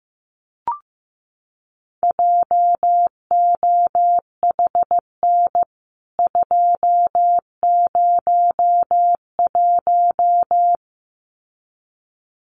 Morse code, heard as a keyed tone, about 15 wpm, 700 Hz. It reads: JOHN 201
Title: CWT Exchange 15wpm